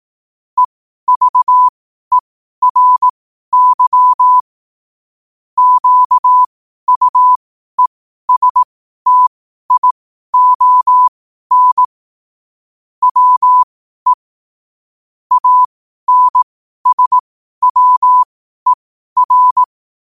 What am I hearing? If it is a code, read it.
Morse code, heard as a keyed tone, about 18 wpm, spaced as for 12 wpm, 1000 Hz. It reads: EVERY QUESTION WE ANSWER
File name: qotdfiles / 12wpm